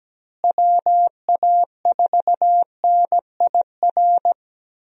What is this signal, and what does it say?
Morse code, heard as a keyed tone, about 17 wpm, 700 Hz. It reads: WA4NIR
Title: USA Call Signs - 6 Characters - Repeated 3x - 17wpm